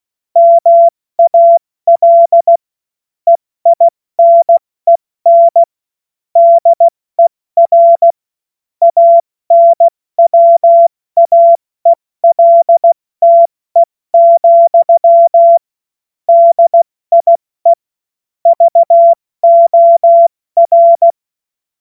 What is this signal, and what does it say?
Morse code, heard as a keyed tone, about 16 wpm, 680 Hz. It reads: MAL EINEN DER ANWAELTE, DIE VOR
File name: Morsekurs / Witze / Witz 31. Mai 2022 70 F80